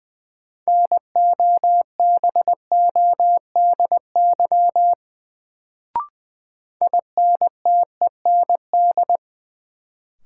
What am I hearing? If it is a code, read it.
Morse code, heard as a keyed tone, about 20 wpm, 700 Hz. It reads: NOBODY INTEND
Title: Sentences from Top 1600 Words - New Word Drill - 20wpm